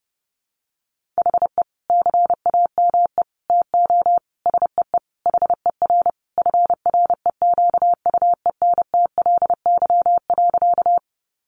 Morse code, heard as a keyed tone, about 30 wpm, 700 Hz. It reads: HE CAME TO SEE HER FREQUENTLY.
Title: Sentences from Top 1300 Words 30wpm